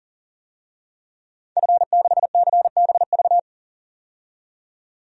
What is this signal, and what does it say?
Morse code, heard as a keyed tone, about 37 wpm, 700 Hz. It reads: F6CBV
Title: DXCC-100 Call Signs - 5 Characters - 40wpm